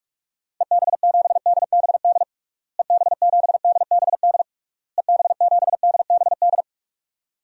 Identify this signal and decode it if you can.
Morse code, heard as a keyed tone, about 41 wpm, 700 Hz. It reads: EB7DBD EB7DBD EB7DBD
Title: DXCC-100 Call Signs - 6 Characters - Repeated 3x - 45wpm